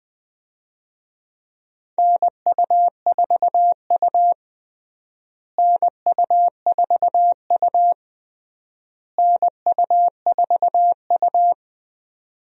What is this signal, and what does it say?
Morse code, heard as a keyed tone, about 20 wpm, 700 Hz. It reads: NU4U NU4U NU4U
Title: USA Call Signs - 4 Characters - Repeated 3x - 20wpm